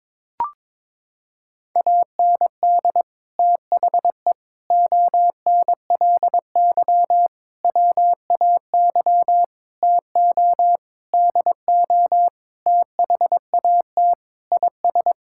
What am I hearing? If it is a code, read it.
Morse code, heard as a keyed tone, about 22 wpm, 700 Hz. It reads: AND THE ONLY WAY TO DO THAT IS
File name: Sets of 9 Words - 22wpm